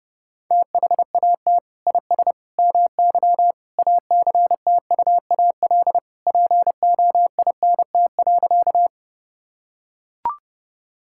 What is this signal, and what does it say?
Morse code, heard as a keyed tone, about 30 wpm, 700 Hz. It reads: THAT IS MY ACTUAL POINT.
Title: Sentences from Top 1000 Words - Review Part 5 - 30wpm